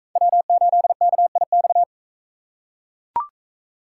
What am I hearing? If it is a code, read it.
Morse code, heard as a keyed tone, about 39 wpm, 700 Hz. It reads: W8KIX